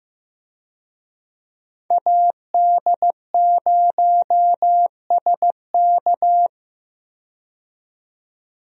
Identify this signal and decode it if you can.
Morse code, heard as a keyed tone, about 15 wpm, 700 Hz. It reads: AD0SK
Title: USA Call Signs - 5 Characters - 15wpm